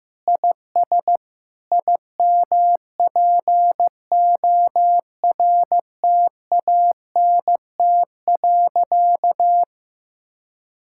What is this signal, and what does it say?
Morse code, heard as a keyed tone, about 16 wpm, 700 Hz. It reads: IS IMPORTANT.
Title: Sentences from Top 1600 Words 15wpm